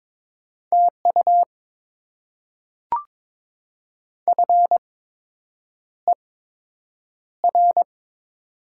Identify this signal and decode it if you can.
Morse code, heard as a keyed tone, about 22 wpm, spaced as for 8 wpm, 700 Hz. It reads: TU FER
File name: Top 20 QSO Elements - 8x Character Spacing - 22wpm